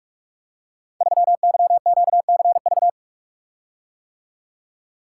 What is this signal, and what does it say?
Morse code, heard as a keyed tone, about 46 wpm, 700 Hz. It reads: 3YQCV
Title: Code Groups - 5 Letter:Numbers - 45wpm